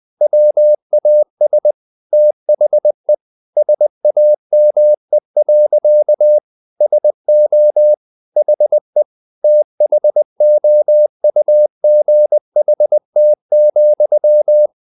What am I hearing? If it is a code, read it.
Morse code, heard as a keyed tone, about 20 wpm, 600 Hz. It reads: WAS THE SAME. SO HE THOUGHT,
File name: the old man and the sea part 2 20wpm